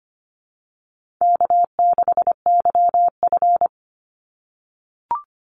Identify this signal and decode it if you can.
Morse code, heard as a keyed tone, about 25 wpm, 700 Hz. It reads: K6YF
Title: Easy Call Signs - 25wpm